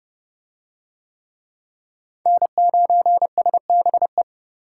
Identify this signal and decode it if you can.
Morse code, heard as a keyed tone, about 29 wpm, 700 Hz. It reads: N9SBE